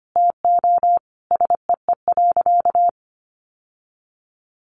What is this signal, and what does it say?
Morse code, heard as a keyed tone, about 25 wpm, 700 Hz. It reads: TO SEE.
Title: Sentences from Top 1200 Words 25wpm